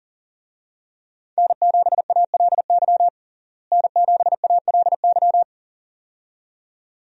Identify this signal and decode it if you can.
Morse code, heard as a keyed tone, about 40 wpm, 700 Hz. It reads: N7ALY N7ALY
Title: USA Call Signs - 5 Characters - Repeated 2x - 40wpm